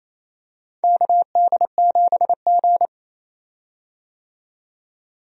Morse code, heard as a keyed tone, about 28 wpm, 700 Hz. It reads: KD7G
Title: USA Call Signs - 4 Characters - 28wpm